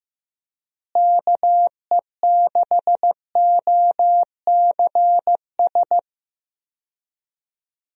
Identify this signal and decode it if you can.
Morse code, heard as a keyed tone, about 15 wpm, 700 Hz. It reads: KE6OCS
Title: USA Call Signs - 6 Characters - Repeated 3x - 15wpm